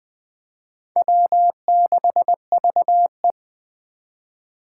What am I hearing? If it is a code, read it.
Morse code, heard as a keyed tone, about 20 wpm, 700 Hz. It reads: W6VE
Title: Speed Racing - USA 1x2 Call Signs 17wpm